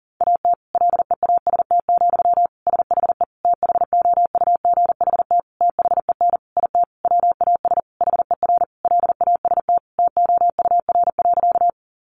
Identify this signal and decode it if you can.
Morse code, heard as a keyed tone, about 40 wpm, 700 Hz. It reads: AT LEAST, SHE THOUGHT THEN IT WAS HER LAST TOUR.
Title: Sentences from Top 1400 Words 40wpm